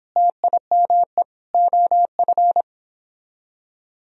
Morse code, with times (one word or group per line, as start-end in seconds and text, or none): 0.2-1.2 TIME
1.5-2.6 OF